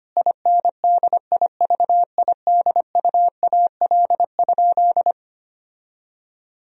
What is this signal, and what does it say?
Morse code, heard as a keyed tone, about 25 wpm, 700 Hz. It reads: INDIVIDUAL?
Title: Sentences from Top 1000 Words - Review Part 4 - 25wpm